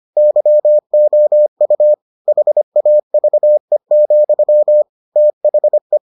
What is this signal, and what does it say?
Morse code, heard as a keyed tone, about 25 wpm, 600 Hz. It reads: YOU HAVE, THE